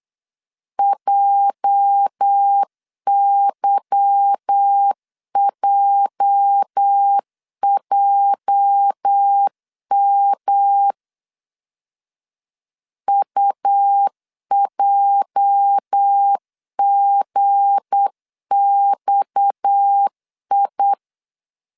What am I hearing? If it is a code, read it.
Morse code, heard as a keyed tone, about 8 wpm, 800 Hz. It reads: JYJJM UJGXI
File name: LEZIONI / 8 YJ45 / 8 YJ 8wpm 3313